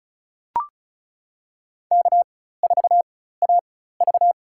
Morse code, heard as a keyed tone, about 36 wpm, spaced as for 16 wpm, 700 Hz. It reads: K4AV
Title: USA Call Signs - 4 Characters - 4x Character Spacing - 35wpm